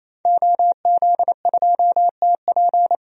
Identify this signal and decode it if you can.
Morse code, heard as a keyed tone, about 28 wpm, 700 Hz. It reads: OZ2TP
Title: DXCC-100 Call Signs - 5 Characters - 28wpm